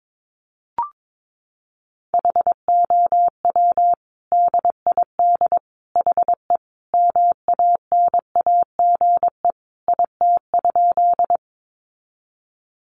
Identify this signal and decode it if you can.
Morse code, heard as a keyed tone, about 22 wpm, 700 Hz. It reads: HOW DID HE MANAGE IT?
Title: Sentences from Top 1400 Words 22wpm